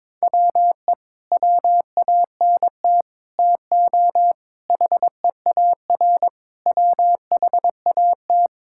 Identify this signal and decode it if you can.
Morse code, heard as a keyed tone, about 22 wpm, 700 Hz. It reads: WE WANT TO HEAR WHAT
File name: Sets of 9 Words - 22wpm